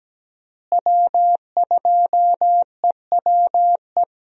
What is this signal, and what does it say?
Morse code, heard as a keyed tone, about 17 wpm, 700 Hz. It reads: W2EWE
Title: USA Call Signs - 5 Characters - Repeated 3x - 17wpm